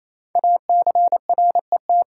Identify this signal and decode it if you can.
Morse code, heard as a keyed tone, about 28 wpm, 700 Hz. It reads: ACRET
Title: Code Groups - 5 Letters - 28wpm